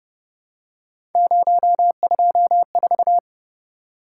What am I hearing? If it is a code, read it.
Morse code, heard as a keyed tone, about 29 wpm, 700 Hz. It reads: 024
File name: Speed-Racing - Three Letter-Number Combinations - 25wpm